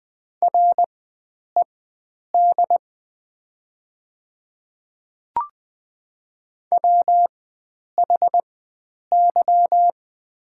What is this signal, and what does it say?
Morse code, heard as a keyed tone, about 20 wpm, spaced as for 9 wpm, 700 Hz. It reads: RED WHY
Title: 4x Character Spacing - Top 500 Words - 20wpm